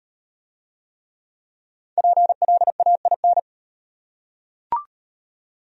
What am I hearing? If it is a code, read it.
Morse code, heard as a keyed tone, about 38 wpm, 700 Hz. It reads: PLAIN